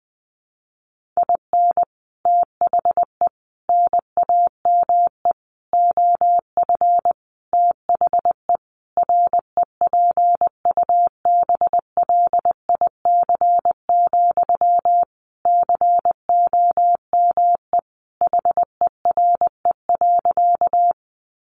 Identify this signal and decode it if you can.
Morse code, heard as a keyed tone, about 19 wpm, 700 Hz. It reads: IN THE NAME OF THE REPUBLIC, COME HERE.
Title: Sentences from Top 1500 Words 20wpm